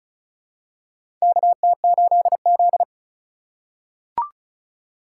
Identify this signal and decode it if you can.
Morse code, heard as a keyed tone, about 34 wpm, 700 Hz. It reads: KT8Z
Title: USA Call Signs - 4 Characters - 4x Character Spacing - 35wpm